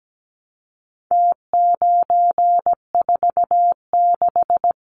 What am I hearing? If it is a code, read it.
Morse code, heard as a keyed tone, about 17 wpm, 700 Hz. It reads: T946
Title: Numbers - Contest Style - 4-Digit 17wpm